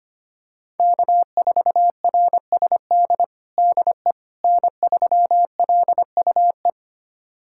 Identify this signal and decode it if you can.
Morse code, heard as a keyed tone, about 25 wpm, 700 Hz. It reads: K4RSD DE N3LUE
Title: USA Callsign DE Callsign - 5 Characters - 25wpm